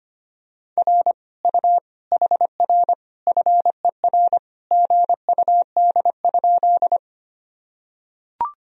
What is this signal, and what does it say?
Morse code, heard as a keyed tone, about 25 wpm, 700 Hz. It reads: R U HR FER GUD?
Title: Sentences from Top 200 Words and Abbreviations 25wpm